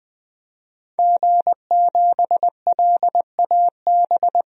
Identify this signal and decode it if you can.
Morse code, heard as a keyed tone, about 20 wpm, 700 Hz. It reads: G7LAB
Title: DXCC-100 Call Signs - 5 Characters - 20wpm